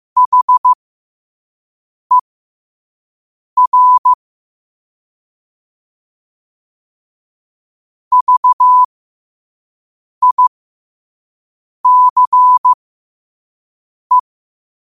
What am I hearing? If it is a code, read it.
Morse code, heard as a keyed tone, about 15 wpm, spaced as for 5 wpm, 1000 Hz. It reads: HER VICE
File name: qotdfiles / 5wpm